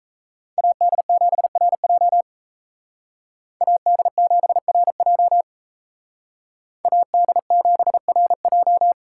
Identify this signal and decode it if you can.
Morse code, heard as a keyed tone, about 36 wpm, 700 Hz. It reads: AD7RJ AD7RJ AD7RJ